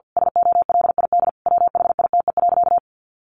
Morse code, heard as a keyed tone, about 50 wpm, 700 Hz. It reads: SOLID WHITE.
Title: Sentences from Top 1500 Words 50wpm